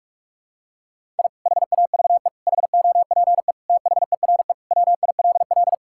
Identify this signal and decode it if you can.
Morse code, heard as a keyed tone, about 41 wpm, 700 Hz. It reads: I HAVE HOPE THERE WILL
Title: Sentences from Top 1000 Words - Review Part 5 - 45wpm